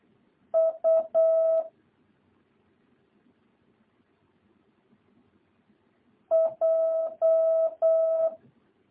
Morse code, heard as a keyed tone, about 8 wpm, 650 Hz. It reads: U J